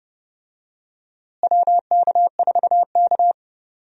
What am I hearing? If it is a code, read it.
Morse code, heard as a keyed tone, about 30 wpm, 700 Hz. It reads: WK4K
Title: USA Call Signs - 4 Characters - Repeated 3x - 30wpm